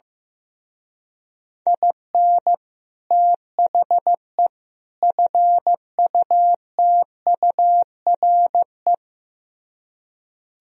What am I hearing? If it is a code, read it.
Morse code, heard as a keyed tone, about 15 wpm, 700 Hz. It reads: IN THE FUTURE
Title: Speed-Racing - Sets of 3 Words 15wpm